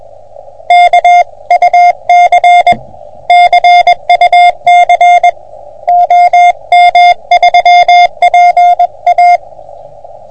Morse code, heard as a keyed tone, about 21 wpm, 720 Hz. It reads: KUC CUC OM3PA
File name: OM3PA PILUP CUC 388